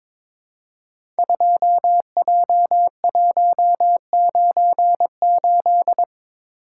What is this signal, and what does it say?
Morse code, heard as a keyed tone, about 22 wpm, 700 Hz. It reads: 2J198
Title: Code Groups - 5 Letter:Numbers - 22wpm